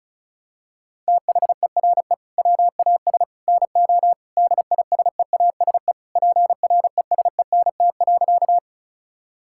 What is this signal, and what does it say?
Morse code, heard as a keyed tone, about 36 wpm, 700 Hz. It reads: THERE WAS NO DISEASE PRESENT.
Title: Sentences from Top 1000 Words - Review Part 6 - 35wpm